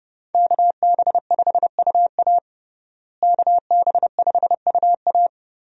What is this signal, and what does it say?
Morse code, heard as a keyed tone, about 31 wpm, 700 Hz. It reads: KB5UA KB5UA
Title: USA Call Signs - 5 Characters - Repeated 3x - 30wpm